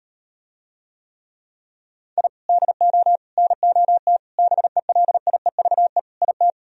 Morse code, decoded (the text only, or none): I DO NOT BELIEVE IT